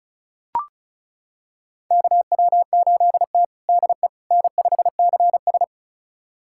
Morse code, heard as a keyed tone, about 34 wpm, 700 Hz. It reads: KW8T DE N5CS
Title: USA Callsign DE Callsign - 4 Characters - 35wpm